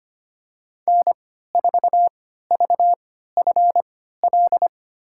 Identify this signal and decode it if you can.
Morse code, heard as a keyed tone, about 25 wpm, spaced as for 14 wpm, 700 Hz. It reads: N4VFL
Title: USA Call Signs - 5 Characters - 3x Character Spacing - 25wpm